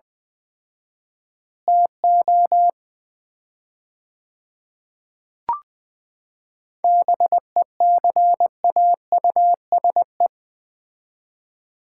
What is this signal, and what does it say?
Morse code, heard as a keyed tone, about 21 wpm, 700 Hz. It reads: TO BECAUSE
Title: 4x Word-Spacing - Sets of 3 Words - 20wpm